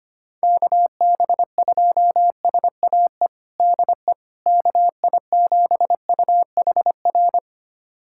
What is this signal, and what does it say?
Morse code, heard as a keyed tone, about 25 wpm, 700 Hz. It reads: KB2SAE DE KI7UHR
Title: USA Callsign DE Callsign - 6 Characters - 25wpm